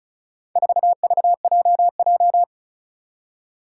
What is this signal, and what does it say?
Morse code, heard as a keyed tone, about 36 wpm, 700 Hz. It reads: 4VJJ
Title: Four Challenging Letter-Number Combinations 35wpm